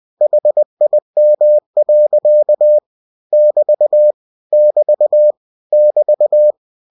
Morse code, heard as a keyed tone, about 20 wpm, 600 Hz. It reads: HIM. <BT> <BT> <BT>